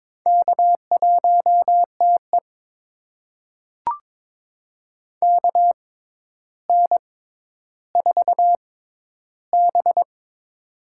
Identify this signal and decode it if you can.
Morse code, heard as a keyed tone, about 22 wpm, spaced as for 11 wpm, 700 Hz. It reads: K1TE KN4B
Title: USA Call Signs - 4 Characters - 6x Character Spacing - 22wpm